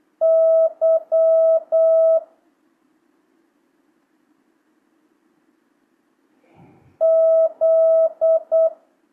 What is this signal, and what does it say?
Morse code, heard as a keyed tone, about 8 wpm, 650 Hz. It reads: Y Z
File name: jh8cbh / 2025cw